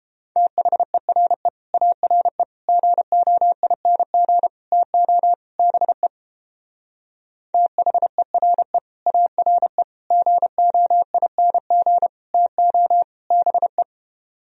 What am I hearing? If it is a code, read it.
Morse code, heard as a keyed tone, about 29 wpm, 700 Hz. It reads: THERE ARE GOING TO BE THERE ARE GOING TO BE